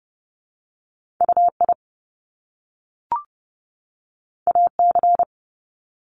Morse code, heard as a keyed tone, about 30 wpm, 700 Hz. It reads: UI AC
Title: Top Two Letter Combinations in Words 30wpm